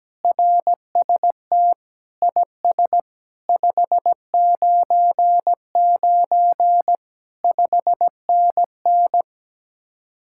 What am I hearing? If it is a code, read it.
Morse code, heard as a keyed tone, about 17 wpm, 700 Hz. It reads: RST IS 599 5NN
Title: Speed-Racing Common QSO Phrases - 17wpm